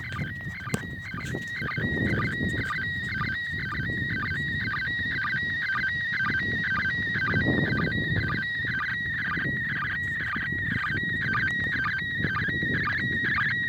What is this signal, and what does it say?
SSTV (slow-scan television), PD 120.
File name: 08-ARISS-SSTV-Expedition-63-ARISS-Series-16-Satellites